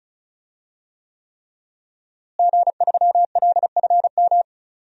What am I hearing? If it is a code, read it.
Morse code, heard as a keyed tone, about 34 wpm, 700 Hz. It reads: G3LFM
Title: DXCC-100 Call Signs - 5 Characters - Repeated 3x - 35wpm